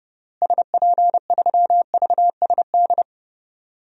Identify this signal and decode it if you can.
Morse code, heard as a keyed tone, about 30 wpm, 700 Hz. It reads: SP3VSD
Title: DXCC-100 Call Signs - 6 Characters - Repeated 3x - 30wpm